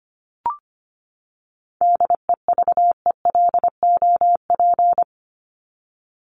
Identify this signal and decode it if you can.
Morse code, heard as a keyed tone, about 25 wpm, 700 Hz. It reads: DEVELOP
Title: ED Suffix - Top 100 Words with Base Words 25wpm